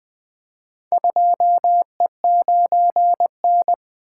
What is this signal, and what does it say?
Morse code, heard as a keyed tone, about 20 wpm, 700 Hz. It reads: 2E9N
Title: Four Letter-Number Combinations - 20wpm